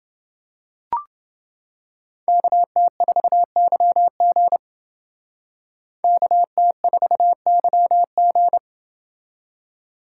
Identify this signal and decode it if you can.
Morse code, heard as a keyed tone, about 29 wpm, 700 Hz. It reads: KT4YG KT4YG